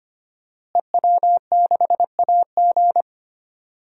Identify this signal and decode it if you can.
Morse code, heard as a keyed tone, about 25 wpm, 700 Hz. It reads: EW6AG
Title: DXCC-100 Call Signs - 5 Characters - Repeated 3x - 25wpm